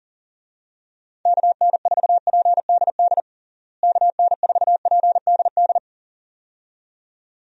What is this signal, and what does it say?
Morse code, heard as a keyed tone, about 37 wpm, 700 Hz. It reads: KN4PDD KN4PDD
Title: USA Call Signs - 6 Characters - Repeated 2x - 40wpm